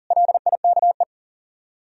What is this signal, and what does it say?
Morse code, heard as a keyed tone, about 52 wpm, 700 Hz. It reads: LIKE